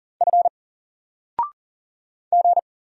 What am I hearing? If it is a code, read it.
Morse code, heard as a keyed tone, about 40 wpm, 700 Hz. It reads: F G
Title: Single Letter-Number 40wpm